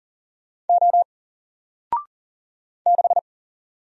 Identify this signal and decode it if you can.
Morse code, heard as a keyed tone, about 40 wpm, 700 Hz. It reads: O 6
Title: Single Letter-Number 40wpm